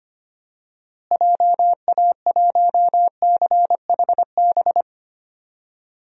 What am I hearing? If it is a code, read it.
Morse code, heard as a keyed tone, about 25 wpm, 700 Hz. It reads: JA1CHB
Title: DXCC-100 Call Signs - 6 Characters - Repeated 3x - 25wpm